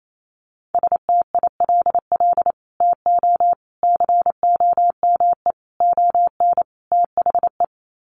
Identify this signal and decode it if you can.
Morse code, heard as a keyed tone, about 28 wpm, 700 Hz. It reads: STILL TO COME ON THE
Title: Sets of 5 Words Encore2 28wpm